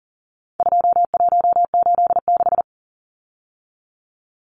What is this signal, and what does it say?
Morse code, heard as a keyed tone, about 39 wpm, 700 Hz. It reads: 2186
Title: Numbers - 4-Digit 40wpm